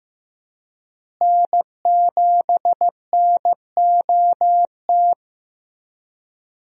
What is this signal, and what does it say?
Morse code, heard as a keyed tone, about 15 wpm, 700 Hz. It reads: N7NOT
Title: USA Call Signs - 5 Characters - Repeated 3x - 15wpm